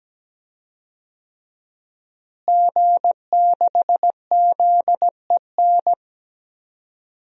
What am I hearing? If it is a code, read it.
Morse code, heard as a keyed tone, about 17 wpm, 700 Hz. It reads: G6ZEN
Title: DXCC-100 Call Signs - 5 Characters - Repeated 3x - 17wpm